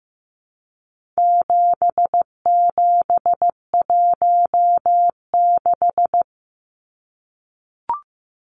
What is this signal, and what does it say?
Morse code, heard as a keyed tone, about 15 wpm, 700 Hz. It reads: 7716
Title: Numbers - 4-Digit 15wpm